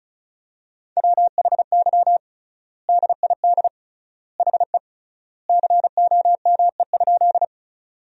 Morse code, heard as a keyed tone, about 34 wpm, 700 Hz. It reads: WHY DID HE COME?
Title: 3x Word-Spacing - Sentences from Top 200 Words - 35wpm